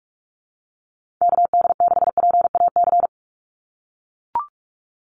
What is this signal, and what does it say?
Morse code, heard as a keyed tone, about 44 wpm, 700 Hz. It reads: KD6PAC
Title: USA Call Signs 45wpm